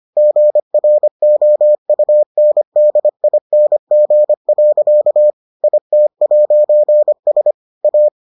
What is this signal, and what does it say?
Morse code, heard as a keyed tone, about 25 wpm, 600 Hz. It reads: GROUNDING. IT'S A